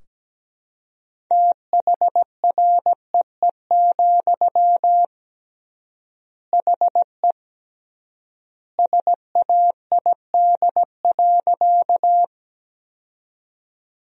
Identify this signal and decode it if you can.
Morse code, heard as a keyed tone, about 17 wpm, 700 Hz. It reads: THREE, HE SAID.
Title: 3x Word-Spacing - Sentences from Top 200 Words - 17wpm